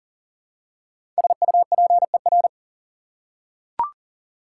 Morse code, heard as a keyed tone, about 40 wpm, 700 Hz. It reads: SUPER
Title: Prefixes and Suffixes 40wpm